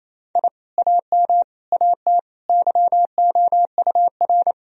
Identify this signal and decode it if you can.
Morse code, heard as a keyed tone, about 27 wpm, 700 Hz. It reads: I AM AT YOUR
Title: Sentences from Top 1000 Words - Review Part 4 - 28wpm